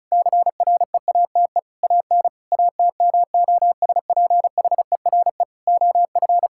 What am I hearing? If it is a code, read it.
Morse code, heard as a keyed tone, about 36 wpm, 700 Hz. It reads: CREATE AN ATMOSPHERE OF
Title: Sentences from Top 1600 Words 35wpm